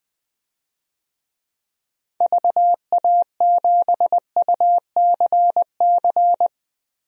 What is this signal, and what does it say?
Morse code, heard as a keyed tone, about 20 wpm, 700 Hz. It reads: VA7UCC
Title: DXCC-100 Call Signs - 6 Characters - Repeated 2x - 20wpm